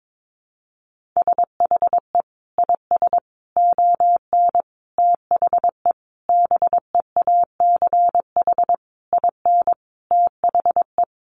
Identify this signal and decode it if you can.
Morse code, heard as a keyed tone, about 22 wpm, 700 Hz. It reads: SHE IS ON THE BEACH IN THE
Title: Sentences from Top 1500 Words 22wpm